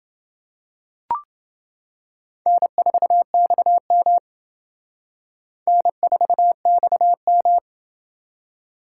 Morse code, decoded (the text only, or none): N4XM N4XM